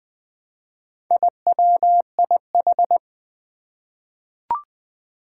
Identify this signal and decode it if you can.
Morse code, heard as a keyed tone, about 20 wpm, 700 Hz. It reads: IWIH
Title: Four Challenging Letter-Number Combinations 20wpm